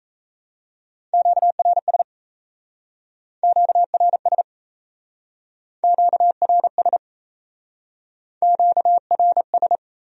Morse code, heard as a keyed tone, about 36 wpm, 700 Hz. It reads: QRS QRS QRS QRS